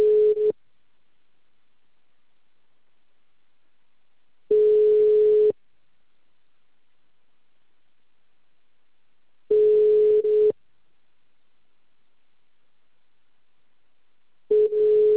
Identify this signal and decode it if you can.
Ringback tone (Europe, 425 Hz).